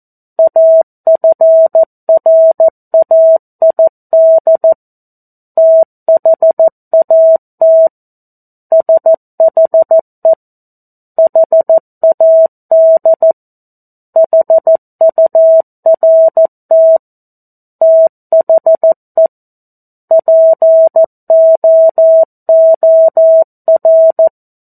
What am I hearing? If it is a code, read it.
Morse code, heard as a keyed tone, about 14 wpm, 650 Hz. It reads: AFRAID THAT SHE HAD HURT THE POOR
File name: 02-Chapter-02 14 WPM